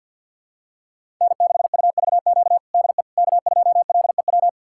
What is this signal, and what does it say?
Morse code, heard as a keyed tone, about 50 wpm, 700 Hz. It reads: N6UVQ DE K2LEW
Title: USA Callsign DE Callsign - 5 Characters - 50wpm